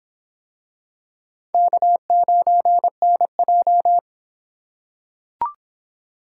Morse code, heard as a keyed tone, about 26 wpm, 700 Hz. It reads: K9NJ